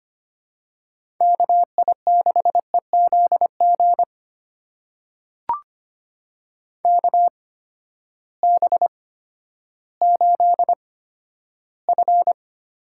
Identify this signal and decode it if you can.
Morse code, heard as a keyed tone, about 25 wpm, spaced as for 12 wpm, 700 Hz. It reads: KI6EZG KB8F
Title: USA Call Signs - 6 Characters - 8x Character Spacing - 25wpm